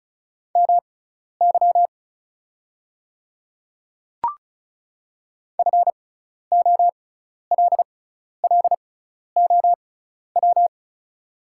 Morse code, heard as a keyed tone, about 36 wpm, spaced as for 12 wpm, 700 Hz. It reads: MY FOLLOW
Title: 6x Character Spacing - Top 200 Words - 35wpm